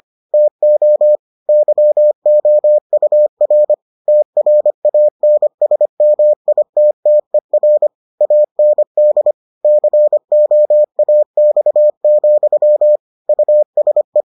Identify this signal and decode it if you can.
Morse code, heard as a keyed tone, about 25 wpm, 600 Hz. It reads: TO YOUR TRANSMITTER AND COAX, USE